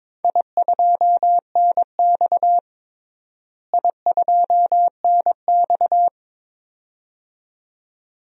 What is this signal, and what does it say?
Morse code, heard as a keyed tone, about 22 wpm, 700 Hz. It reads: I2NX I2NX
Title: DXCC-100 Call Signs - 4 Characters - Repeated 2x - 22wpm